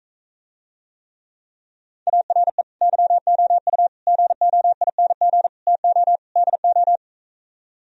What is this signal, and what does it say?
Morse code, heard as a keyed tone, about 45 wpm, 700 Hz. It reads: ARE YOU GOING TO DO